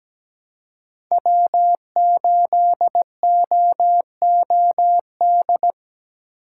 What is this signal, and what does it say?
Morse code, heard as a keyed tone, about 17 wpm, 700 Hz. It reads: W8OOD
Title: USA Call Signs - 5 Characters - 3x Character Spacing - 17wpm